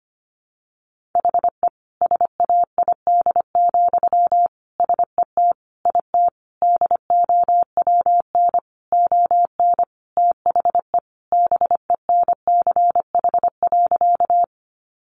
Morse code, heard as a keyed tone, about 25 wpm, 700 Hz. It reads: HE SAID, SET IT DOWN ON THE BENCH.
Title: Sentences from Top 1400 Words 25wpm